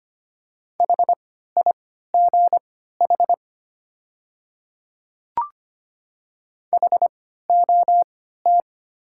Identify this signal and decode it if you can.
Morse code, heard as a keyed tone, about 25 wpm, spaced as for 14 wpm, 700 Hz. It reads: HIGH HOT
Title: 3x Character Spacing - Top 200 Words - 25wpm